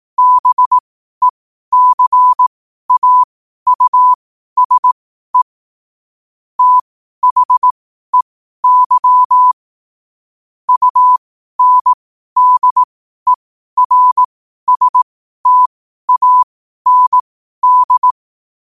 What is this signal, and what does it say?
Morse code, heard as a keyed tone, about 18 wpm, spaced as for 12 wpm, 1000 Hz. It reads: BECAUSE THEY UNDERSTAND
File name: qotdfiles / 12wpm